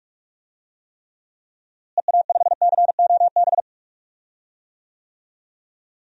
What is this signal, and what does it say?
Morse code, heard as a keyed tone, about 46 wpm, 700 Hz. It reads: EA5COB
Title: DXCC-100 Call Signs - 6 Characters - Repeated 3x - 45wpm